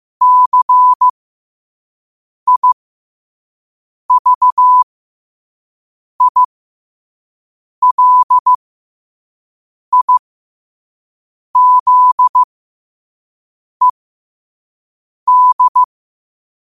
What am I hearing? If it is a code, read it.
Morse code, heard as a keyed tone, about 15 wpm, spaced as for 5 wpm, 1000 Hz. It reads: CIVILIZED
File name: qotdfiles / 5wpm